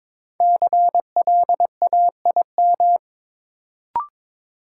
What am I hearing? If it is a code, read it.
Morse code, heard as a keyed tone, about 22 wpm, 700 Hz. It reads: CLAIM
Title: Sentences from Top 1500 Words - New Word Drill Alt - 22wpm